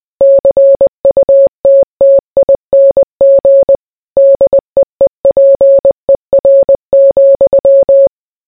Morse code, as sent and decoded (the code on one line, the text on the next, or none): -.-. ..- - - .. -. --. / -.. . .--. .-. --..--
CUTTING DEEPER,